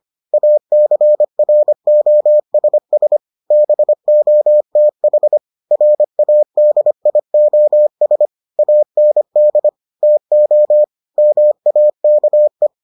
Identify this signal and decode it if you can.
Morse code, heard as a keyed tone, about 25 wpm, 600 Hz. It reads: ACROSS BOTH RADIOS AND TO MAKE